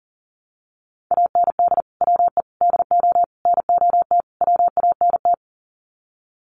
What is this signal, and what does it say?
Morse code, heard as a keyed tone, about 40 wpm, 700 Hz. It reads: AND WE DO NOT WANT
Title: Sets of 5 Words Encore2 40wpm